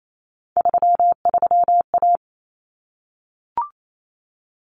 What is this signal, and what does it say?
Morse code, heard as a keyed tone, about 28 wpm, 700 Hz. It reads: 33A
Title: Common Cut Numbers - 3-Digit 28wpm